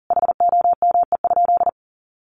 Morse code, heard as a keyed tone, about 40 wpm, 700 Hz. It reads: HOME?
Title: Sentences from Top 1000 Words 40wpm